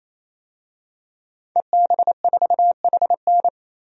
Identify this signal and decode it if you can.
Morse code, heard as a keyed tone, about 28 wpm, 700 Hz. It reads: EB4HN
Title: DXCC-100 Call Signs - 5 Characters - 28wpm